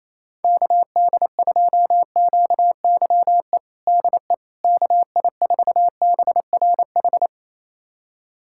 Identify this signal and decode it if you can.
Morse code, heard as a keyed tone, about 28 wpm, 700 Hz. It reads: KD2QYE DE KI4BRH
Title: USA Callsign DE Callsign 28wpm